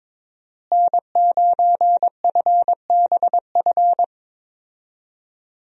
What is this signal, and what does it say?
Morse code, heard as a keyed tone, about 22 wpm, 700 Hz. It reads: N9FBF
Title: USA Call Signs - 5 Characters - Repeated 2x - 22wpm